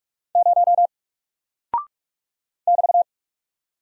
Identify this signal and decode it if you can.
Morse code, heard as a keyed tone, about 44 wpm, 700 Hz. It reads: 0 <BT>
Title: Single Character (Letter:Number:Punctuation:Prosign) Alt 45wpm